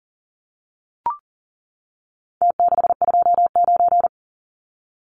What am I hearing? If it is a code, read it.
Morse code, heard as a keyed tone, about 40 wpm, 700 Hz. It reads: T629
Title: Numbers - Contest Style - 4-Digit 40wpm